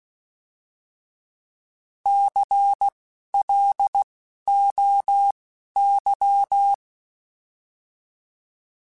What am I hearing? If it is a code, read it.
Morse code, heard as a keyed tone, about 16 wpm, spaced as for 11 wpm, 800 Hz. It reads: CLOY
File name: Alphabetical / Alphabetical mp3s / Cs @10